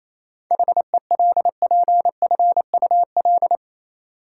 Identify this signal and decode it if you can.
Morse code, heard as a keyed tone, about 28 wpm, 700 Hz. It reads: HELPFUL
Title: Seven Letter Words - 28wpm